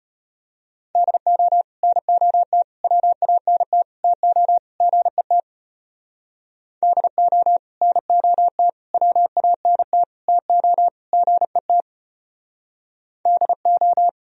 Morse code, heard as a keyed tone, about 31 wpm, 700 Hz. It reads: DO NOT WANT TO GET DO NOT WANT TO GET DO